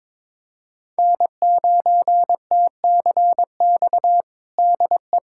Morse code, heard as a keyed tone, about 22 wpm, 700 Hz. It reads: N9TCX DE
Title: USA Callsign DE Callsign 22wpm